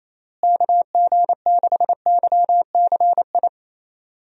Morse code, heard as a keyed tone, about 28 wpm, 700 Hz. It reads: KG6YCI
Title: USA Call Signs - 6 Characters - Repeated 3x - 28wpm